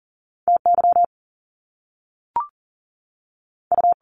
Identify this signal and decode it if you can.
Morse code, heard as a keyed tone, about 40 wpm, 700 Hz. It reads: T Y U
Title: Top Two Letter Combinations in Words 40wpm